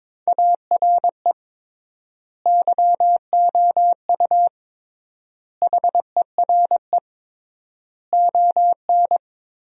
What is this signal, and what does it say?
Morse code, heard as a keyed tone, about 23 wpm, 700 Hz. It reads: ARE YOU HERE ON
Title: 3x Word-Spacing - Sentences from Top 400 Words - 22wpm